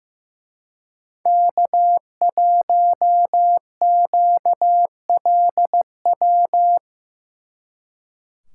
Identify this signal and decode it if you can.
Morse code, heard as a keyed tone, about 18 wpm, 700 Hz. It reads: K1QLW
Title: Speed Racing - USA 1x3 Call Signs 15wpm